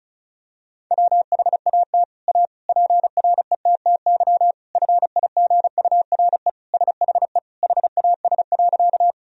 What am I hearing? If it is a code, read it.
Morse code, heard as a keyed tone, about 34 wpm, 700 Hz. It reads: WHAT A PRETTY FIGURE SHE HAS.
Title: Sentences from Top 1000 Words - Review Part 2 - 35wpm